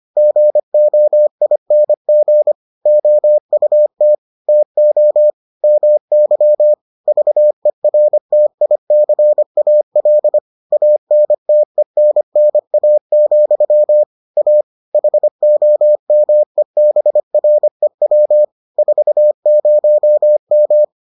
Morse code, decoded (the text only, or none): GOING OUT TO MY VERTICAL ANTENNA, A HOMEBREW 40M